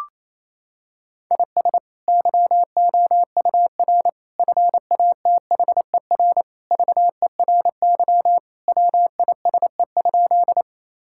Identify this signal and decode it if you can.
Morse code, heard as a keyed tone, about 28 wpm, 700 Hz. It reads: IS YOUR FATHER VERY WISE?
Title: Sentences from Top 1600 Words 28wpm